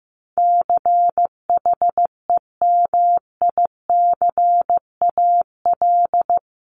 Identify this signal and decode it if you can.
Morse code, heard as a keyed tone, about 15 wpm, 700 Hz. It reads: CHEMICAL
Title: Top 1001-1250 Words 15wpm